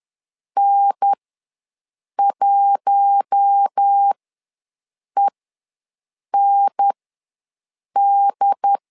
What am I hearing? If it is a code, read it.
Morse code, heard as a keyed tone, about 10 wpm, spaced as for 6 wpm, 800 Hz. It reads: N1END
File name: LEZIONI / 4 Q1 ws 10wpm 2538